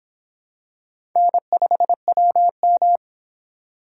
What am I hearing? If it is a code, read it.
Morse code, heard as a keyed tone, about 25 wpm, 700 Hz. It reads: N5WM